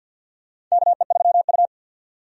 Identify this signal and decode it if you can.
Morse code, heard as a keyed tone, about 49 wpm, 700 Hz. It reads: KE3U
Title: USA Call Signs - 4 Characters - 50wpm